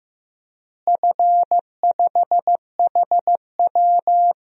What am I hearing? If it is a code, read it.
Morse code, heard as a keyed tone, about 15 wpm, 700 Hz. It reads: F5HW
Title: DXCC-100 Call Signs - 4 Characters - Repeated 3x - 15wpm